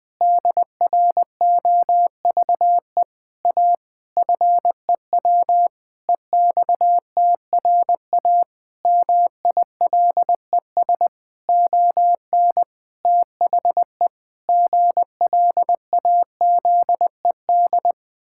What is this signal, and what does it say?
Morse code, heard as a keyed tone, about 20 wpm, 700 Hz. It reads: DROVE A FEW EXTRA MILES ON THE GLAZED
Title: Pangram Challenge v2 - 20wpm